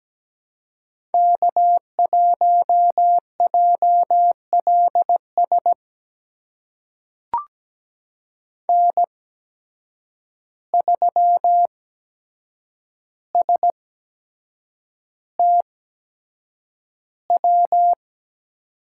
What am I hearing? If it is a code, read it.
Morse code, heard as a keyed tone, about 17 wpm, spaced as for 7 wpm, 700 Hz. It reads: K1JLS N3STW
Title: USA Call Signs - 5 Characters - 8x Character Spacing - 17wpm